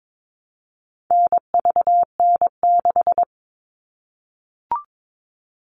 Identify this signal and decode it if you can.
Morse code, heard as a keyed tone, about 22 wpm, 700 Hz. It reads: NVN6
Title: All Cut Numbers - 4-Digit 22wpm